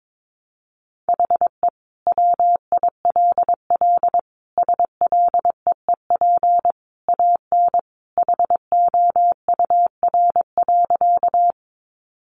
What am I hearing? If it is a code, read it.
Morse code, heard as a keyed tone, about 22 wpm, 700 Hz. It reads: HE WILL SLEEP AN HOUR.
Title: Sentences from Top 500 Words 22wpm